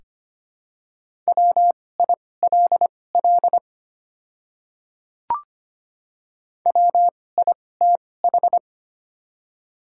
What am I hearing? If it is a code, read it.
Morse code, heard as a keyed tone, about 25 wpm, spaced as for 18 wpm, 690 Hz. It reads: WILL WITH
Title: 2x Character Spacing - Top 100 Words - 25wpm